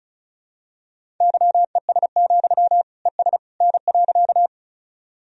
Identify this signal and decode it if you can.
Morse code, heard as a keyed tone, about 36 wpm, 700 Hz. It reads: YES, ES N.
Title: Sentences from Top 500 Words and Abbreviations 35wpm